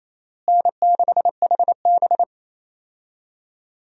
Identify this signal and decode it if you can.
Morse code, heard as a keyed tone, about 27 wpm, 700 Hz. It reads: N6HB
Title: USA Call Signs - 4 Characters - Repeated 2x - 28wpm